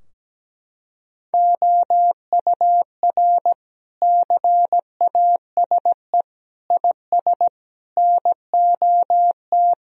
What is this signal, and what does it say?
Morse code, heard as a keyed tone, about 17 wpm, 700 Hz. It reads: OUR CASE IS NOT
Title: Sentences from Top 1000 Words - Review Part 2 - 17wpm